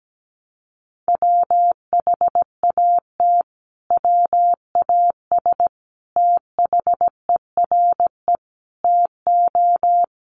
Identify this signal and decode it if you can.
Morse code, heard as a keyed tone, about 17 wpm, 700 Hz. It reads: WHAT WAS THERE TO
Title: Sentences from Top 1100 Words 17wpm